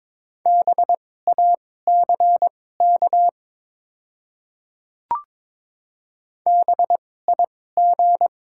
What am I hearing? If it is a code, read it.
Morse code, heard as a keyed tone, about 22 wpm, spaced as for 16 wpm, 700 Hz. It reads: BACK BIG
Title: 2x Character Spacing - Top 300 Words - 22wpm